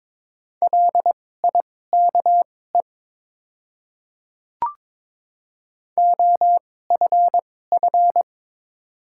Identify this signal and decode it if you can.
Morse code, heard as a keyed tone, about 23 wpm, spaced as for 16 wpm, 700 Hz. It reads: LIKE OFF
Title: 2x Character Spacing - Top 300 Words - 22wpm